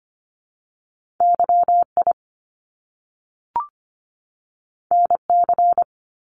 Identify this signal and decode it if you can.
Morse code, heard as a keyed tone, about 25 wpm, 700 Hz. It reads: YI NC
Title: Top Two Letter Combinations in Words 25wpm